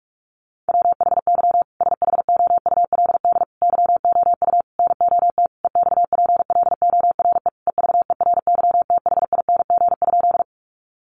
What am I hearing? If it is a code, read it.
Morse code, heard as a keyed tone, about 44 wpm, 700 Hz. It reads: WHY SHOULD YOU NOT EXPLORE EVERYTHING?
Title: Sentences from Top 1100 Words 45wpm